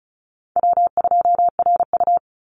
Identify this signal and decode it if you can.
Morse code, heard as a keyed tone, about 34 wpm, 700 Hz. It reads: W2RU
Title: Easy Call Signs Repeated 3x - 35wpm